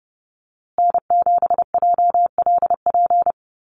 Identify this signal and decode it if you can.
Morse code, heard as a keyed tone, about 30 wpm, 700 Hz. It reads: N7JLP
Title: USA Call Signs Repeated 2x 30wpm